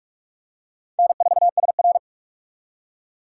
Morse code, heard as a keyed tone, about 47 wpm, 700 Hz. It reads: N4SR
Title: K1USN SST QSO Practice - 45wpm